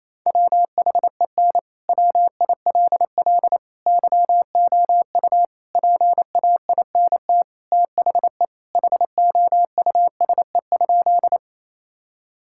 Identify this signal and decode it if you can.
Morse code, heard as a keyed tone, about 27 wpm, 700 Hz. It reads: WHEN WILL YOU PAINT THE HOUSE?
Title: Sentences from Top 1000 Words - Review Part 6 - 28wpm